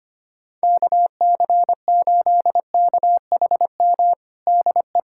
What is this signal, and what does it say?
Morse code, heard as a keyed tone, about 25 wpm, 700 Hz. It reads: KC8KHM DE
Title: USA Callsign DE Callsign - 6 Characters - 25wpm